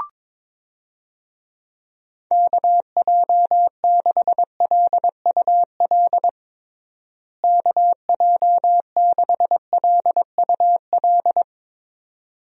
Morse code, heard as a keyed tone, about 21 wpm, 700 Hz. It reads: KJ6LUL KJ6LUL
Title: USA Call Signs - 6 Characters - Repeated 2x - 22wpm